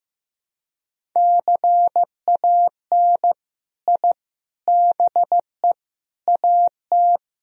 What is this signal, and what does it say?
Morse code, heard as a keyed tone, about 16 wpm, 700 Hz. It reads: CAN I BE AT
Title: Sentences from Top 1000 Words - Review Part 1 - 15wpm